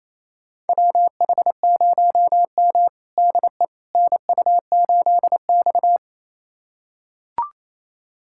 Morse code, heard as a keyed tone, about 27 wpm, 700 Hz. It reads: WH0M DE NU8X
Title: USA Callsign DE Callsign - 4 Characters - 28wpm